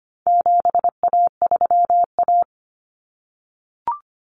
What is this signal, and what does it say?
Morse code, heard as a keyed tone, about 25 wpm, 700 Hz. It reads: 7A3A
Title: Common Cut Numbers - 4-Digit 25wpm